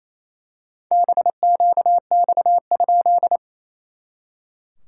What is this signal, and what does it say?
Morse code, heard as a keyed tone, about 27 wpm, 690 Hz. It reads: BQX?
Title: SOTA QSO Practice 28wpm